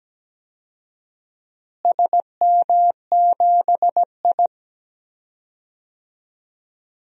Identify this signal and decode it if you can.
Morse code, heard as a keyed tone, about 17 wpm, 700 Hz. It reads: SM7I
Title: DXCC-100 Call Signs - 4 Characters - Repeated 2x - 17wpm